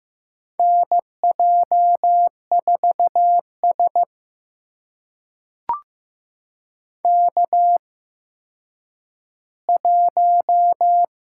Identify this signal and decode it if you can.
Morse code, heard as a keyed tone, about 15 wpm, spaced as for 9 wpm, 700 Hz. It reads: NJ4S K1
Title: USA Call Signs - 4 Characters - 8x Character Spacing - 15wpm